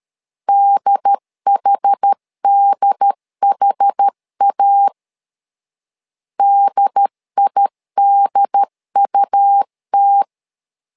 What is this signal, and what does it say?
Morse code, heard as a keyed tone, about 13 wpm, 800 Hz. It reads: DHDHA DIDUT
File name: LEZIONI / 2 UHDN / 2 UHD 12wpm 1540